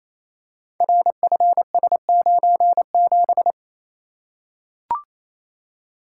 Morse code, heard as a keyed tone, about 28 wpm, 700 Hz. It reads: RFS97
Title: Code Groups - 5 Letter:Numbers - 28wpm